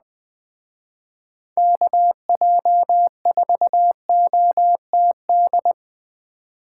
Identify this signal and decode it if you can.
Morse code, heard as a keyed tone, about 20 wpm, 700 Hz. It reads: KJ4OTD